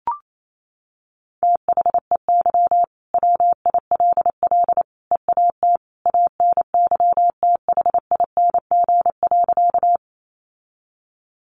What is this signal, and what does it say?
Morse code, heard as a keyed tone, about 28 wpm, 700 Hz. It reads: THEY WILL EAT ANYTHING.
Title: Sentences from Top 1000 Words 28wpm